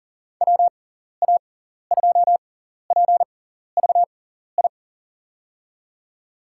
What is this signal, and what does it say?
Morse code, heard as a keyed tone, about 40 wpm, spaced as for 14 wpm, 700 Hz. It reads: WA2PVI
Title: USA Call Signs - 6x Character Spacing - 40wpm